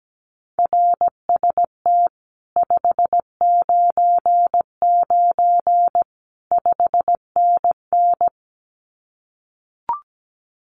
Morse code, heard as a keyed tone, about 17 wpm, 700 Hz. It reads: RST 599 5NN
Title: Common QSO Phrases Encore 17wpm